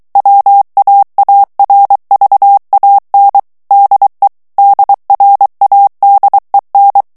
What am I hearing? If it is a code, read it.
Morse code, heard as a keyed tone, about 23 wpm, 800 Hz. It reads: WAARVAN DE DRADEN
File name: HOLLANDAIS texte clair 1 vitesse 22 mots minute